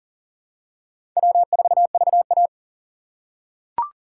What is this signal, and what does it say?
Morse code, heard as a keyed tone, about 40 wpm, 700 Hz. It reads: W4VA
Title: USA Call Signs - 4 Characters - 4x Character Spacing - 40wpm